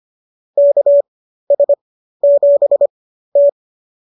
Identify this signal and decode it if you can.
Morse code, heard as a keyed tone, about 25 wpm, spaced as for 13 wpm, 570 Hz. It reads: KS7T